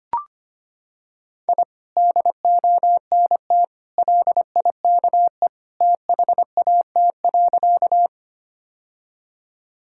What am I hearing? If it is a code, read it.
Morse code, heard as a keyed tone, about 25 wpm, 700 Hz. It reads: I DONT LIKE THAT.
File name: practice / 25wpm / Sentences with 2-Word Phrases - 25wpm